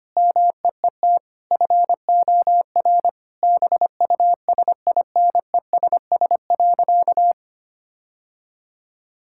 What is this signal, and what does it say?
Morse code, heard as a keyed tone, about 25 wpm, 700 Hz. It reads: MEET FOR BUSINESS.
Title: Sentences from Top 1000 Words - Review Part 6 - 25wpm